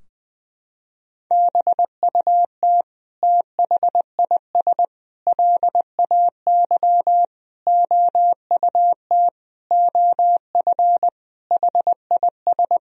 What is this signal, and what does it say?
Morse code, heard as a keyed tone, about 20 wpm, 700 Hz. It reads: BUT THIS LAY OUT OF HIS
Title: Sentences from Top 1000 Words - Review Part 2 - 20wpm